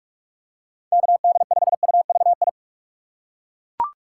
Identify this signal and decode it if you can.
Morse code, heard as a keyed tone, about 41 wpm, 700 Hz. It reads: KD5UVI
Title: USA Call Signs - 4x Character Spacing - 45wpm